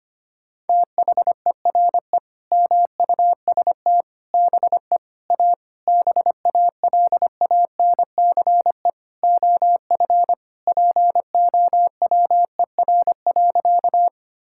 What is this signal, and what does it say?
Morse code, heard as a keyed tone, about 25 wpm, 700 Hz. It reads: THERE MUST BE A BALANCE OF POWER.
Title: Sentences from Top 1000 Words - Review Part 6 - 25wpm